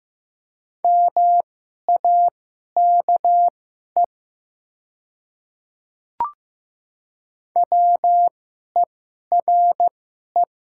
Morse code, heard as a keyed tone, about 15 wpm, spaced as for 11 wpm, 700 Hz. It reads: MAKE WERE
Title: 2x Character Spacing - Top 100 Words - 15wpm